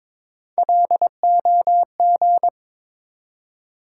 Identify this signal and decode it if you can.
Morse code, heard as a keyed tone, about 22 wpm, 700 Hz. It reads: LOG
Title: Three Letter Words - 22wpm